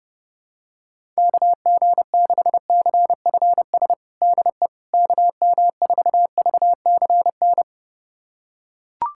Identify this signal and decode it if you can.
Morse code, heard as a keyed tone, about 30 wpm, 700 Hz. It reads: KG6CFS DE KM4VCN
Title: USA Callsign DE Callsign - 6 Characters - 30wpm